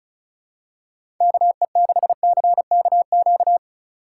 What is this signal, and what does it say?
Morse code, heard as a keyed tone, about 36 wpm, 700 Hz. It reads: KE6CKQ
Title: USA Call Signs - 3x Character Spacing - 35wpm